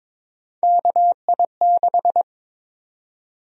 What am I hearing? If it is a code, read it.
Morse code, heard as a keyed tone, about 22 wpm, 700 Hz. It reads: KI6
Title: USA Call Sign Prefixes - 22wpm